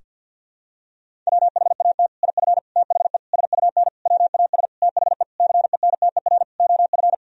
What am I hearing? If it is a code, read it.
Morse code, heard as a keyed tone, about 51 wpm, 700 Hz. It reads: WHAT IF THE SUN WAS THE CENTER OF
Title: Sentences from Top 1000 Words - Review Part 2 - 50wpm